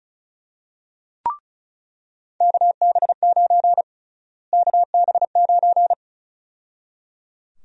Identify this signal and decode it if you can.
Morse code, heard as a keyed tone, about 36 wpm, 700 Hz. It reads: KB9 KB9
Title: USA Call Sign Prefixes - Repeated 2x - 35wpm